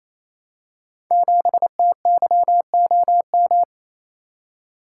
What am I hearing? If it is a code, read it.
Morse code, heard as a keyed tone, about 28 wpm, 700 Hz. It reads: 7TYOM
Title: Code Groups - 5 Letter:Numbers - 28wpm